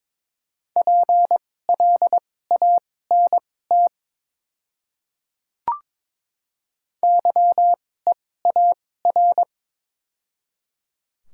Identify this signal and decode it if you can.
Morse code, heard as a keyed tone, about 21 wpm, spaced as for 16 wpm, 700 Hz. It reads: PLANT YEAR
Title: 2x Character Spacing - Top 300 Words - 22wpm